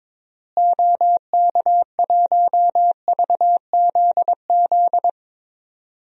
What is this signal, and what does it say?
Morse code, heard as a keyed tone, about 21 wpm, 700 Hz. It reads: OK1VZZ
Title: DXCC-100 Call Signs - 6 Characters - Repeated 2x - 22wpm